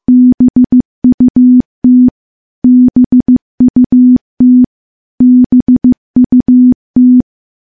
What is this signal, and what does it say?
Morse code, heard as a keyed tone, about 15 wpm, 250 Hz. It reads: BUT BUT BUT
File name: morse / morse-training-15WPM